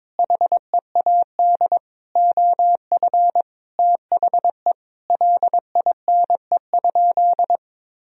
Morse code, heard as a keyed tone, about 22 wpm, 700 Hz. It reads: HEAD OF THE LINE?
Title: Sentences from Top 1000 Words - Review Part 1 - 22wpm